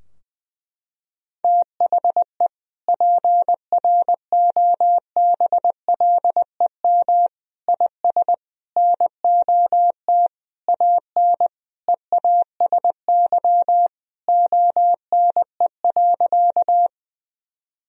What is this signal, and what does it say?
Morse code, heard as a keyed tone, about 20 wpm, 700 Hz. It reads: THE PROBLEM IS NOT AN EASY ONE.
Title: Sentences from Top 1000 Words - Review Part 2 - 20wpm